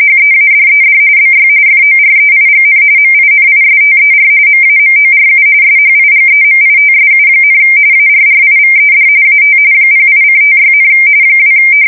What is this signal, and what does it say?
RTTY, about 45.45 baud. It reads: HEADED BY FOREIGN MINISTER MAMORU SHIGEMITSU, ARRIVED AT 08:56, 2
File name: fsk170 / USS-Missouri